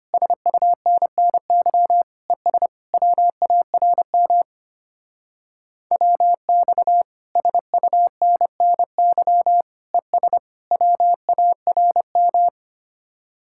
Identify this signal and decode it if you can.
Morse code, heard as a keyed tone, about 31 wpm, 700 Hz. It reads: SUNNY ES WARM WX SUNNY ES WARM